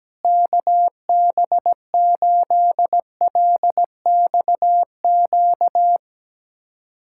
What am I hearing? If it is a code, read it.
Morse code, heard as a keyed tone, about 17 wpm, 700 Hz. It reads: KB8LXQ
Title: USA Callsign DE Callsign - 6 Characters - 17wpm